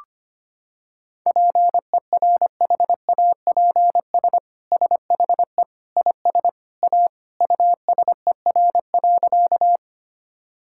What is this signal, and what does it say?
Morse code, heard as a keyed tone, about 25 wpm, 700 Hz. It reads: PERHAPS SHE IS A USER.
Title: Sentences from Top 1000 Words - Review Part 5 - 25wpm